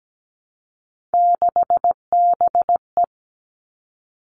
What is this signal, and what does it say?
Morse code, heard as a keyed tone, about 17 wpm, 700 Hz. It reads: 6BE
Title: Three Challenging Letter-Number Combinations 17wpm